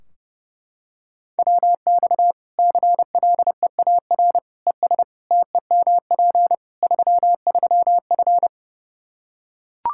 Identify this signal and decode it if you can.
Morse code, heard as a keyed tone, about 32 wpm, 700 Hz. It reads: WX CLEAR ES TEMP 33F
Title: Practice QSO - Exchange 2 - Info - 30wpm